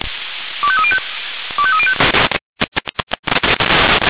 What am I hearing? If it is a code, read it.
Five-tone selective call (ZVEI), heard as quick runs of five tones, tones of about 70 ms. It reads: 25305 25305